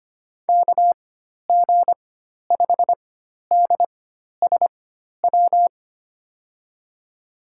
Morse code, heard as a keyed tone, about 25 wpm, spaced as for 12 wpm, 700 Hz. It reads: KG5DSW
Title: USA Call Signs - 6 Characters - 4x Character Spacing - 25wpm